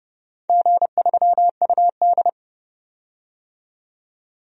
Morse code, heard as a keyed tone, about 31 wpm, 700 Hz. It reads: G3UD
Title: DXCC-100 Call Signs - 4 Characters - 30wpm